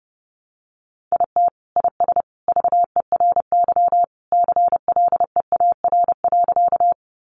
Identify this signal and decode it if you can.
Morse code, heard as a keyed tone, about 30 wpm, 700 Hz. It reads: IT IS VERY CLEAR.
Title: Sentences from Top 500 Words 30wpm